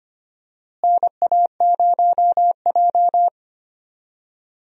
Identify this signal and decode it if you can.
Morse code, heard as a keyed tone, about 25 wpm, 700 Hz. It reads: NA0J
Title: USA Call Signs - 4 Characters - Repeated 2x - 25wpm